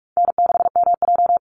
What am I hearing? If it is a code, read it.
Morse code, heard as a keyed tone, about 44 wpm, 700 Hz. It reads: N6MJ
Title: Easy Call Signs - 45wpm